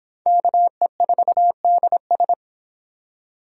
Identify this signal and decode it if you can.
Morse code, heard as a keyed tone, about 25 wpm, 700 Hz. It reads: KE4DS